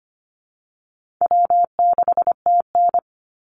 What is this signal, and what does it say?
Morse code, heard as a keyed tone, about 25 wpm, 700 Hz. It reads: W6TN
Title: Easy Call Signs Repeated 3x - 25wpm